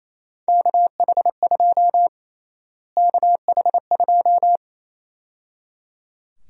Morse code, heard as a keyed tone, about 28 wpm, 700 Hz. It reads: KH2 KH2
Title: USA Call Sign Prefixes - Repeated 2x - 28wpm